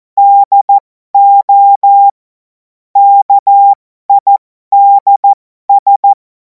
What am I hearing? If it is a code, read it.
Morse code, heard as a keyed tone, about 14 wpm, spaced as for 12 wpm, 800 Hz. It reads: DO KIDS
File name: Morsekurs / Witze